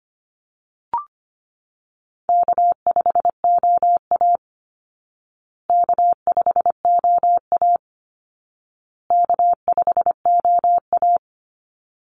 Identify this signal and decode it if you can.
Morse code, heard as a keyed tone, about 25 wpm, 700 Hz. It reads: K5OA K5OA K5OA
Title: Easy Call Signs Repeated 3x - 25wpm